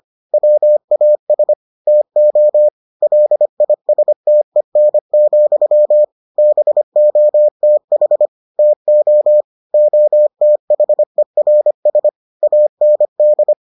Morse code, heard as a keyed tone, about 25 wpm, 600 Hz. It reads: WAS TO LISTEN, BOTH TO OTHERS AND